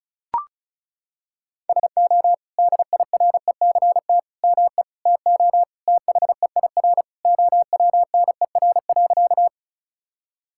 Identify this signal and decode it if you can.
Morse code, heard as a keyed tone, about 34 wpm, 700 Hz. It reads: SO DIRECT ME TO THEIR OWNER.
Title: Sentences from Top 1600 Words 35wpm